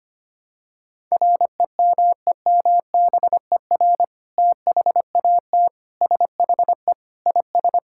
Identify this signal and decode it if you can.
Morse code, heard as a keyed tone, about 25 wpm, 700 Hz. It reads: REMEMBER THAT SHE IS
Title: Sentences from Top 1000 Words - Review Part 6 - 25wpm